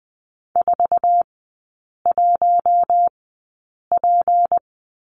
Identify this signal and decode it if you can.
Morse code, heard as a keyed tone, about 20 wpm, spaced as for 8 wpm, 700 Hz. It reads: 41P
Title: 172 Lesson 28e - Number 9 - Three Characters 20wpm